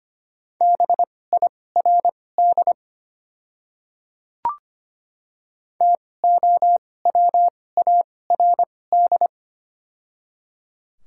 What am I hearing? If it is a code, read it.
Morse code, heard as a keyed tone, about 25 wpm, spaced as for 18 wpm, 700 Hz. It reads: BIRD TOWARD
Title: 2x Character Spacing - Top 500 Words - 25wpm